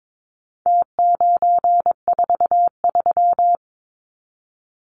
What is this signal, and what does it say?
Morse code, heard as a keyed tone, about 22 wpm, 700 Hz. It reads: T943
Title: Numbers - Contest Style - 4-Digit 22wpm